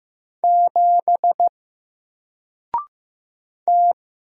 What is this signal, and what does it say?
Morse code, heard as a keyed tone, about 15 wpm, 700 Hz. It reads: 7 T
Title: Single Letter-Number Alt 15wpm